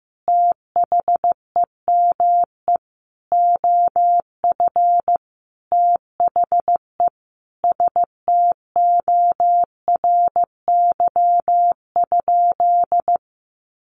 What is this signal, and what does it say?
Morse code, heard as a keyed tone, about 14 wpm, 700 Hz. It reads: THEME OF THE STORY?
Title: Sentences from Top 1100 Words 15wpm